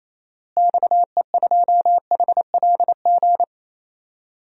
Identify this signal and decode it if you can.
Morse code, heard as a keyed tone, about 28 wpm, 700 Hz. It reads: XE2HLG
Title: DXCC-100 Call Signs - 6 Characters - Repeated 2x - 28wpm